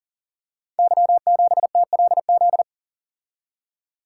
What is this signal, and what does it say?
Morse code, heard as a keyed tone, about 39 wpm, 700 Hz. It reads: Y7TLZ
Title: Code Groups - 5 Letter:Numbers - 40wpm